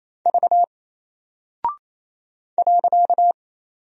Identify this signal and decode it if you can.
Morse code, heard as a keyed tone, about 27 wpm, 700 Hz. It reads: V .
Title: Single Character (Letter:Number:Punctuation:Prosign) Alt 28wpm